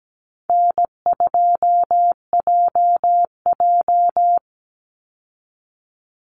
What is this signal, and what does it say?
Morse code, heard as a keyed tone, about 17 wpm, 700 Hz. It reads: N2JJ
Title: Call Signs - 17wpm